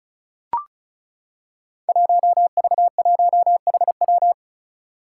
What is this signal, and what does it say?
Morse code, heard as a keyed tone, about 36 wpm, 700 Hz. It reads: 1V1HW
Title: Five Challenging Letter-Number Combinations 35wpm